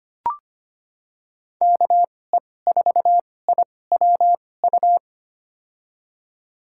Morse code, heard as a keyed tone, about 25 wpm, spaced as for 18 wpm, 700 Hz. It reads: KE4IWU
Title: USA Call Signs - 6 Characters - 2x Character Spacing - 25wpm